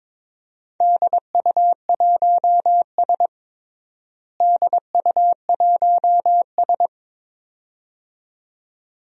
Morse code, heard as a keyed tone, about 22 wpm, 700 Hz. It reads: DU1S DU1S
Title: DXCC-100 Call Signs - 4 Characters - Repeated 2x - 22wpm